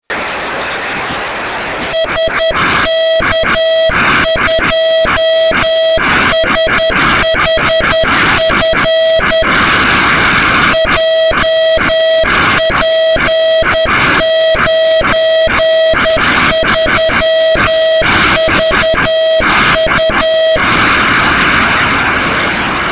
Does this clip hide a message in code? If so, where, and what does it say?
Yes: Morse code from 1.9 to 20.6 s, heard as a keyed tone, about 10 wpm, 650 Hz. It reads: SK2SHF JP93VU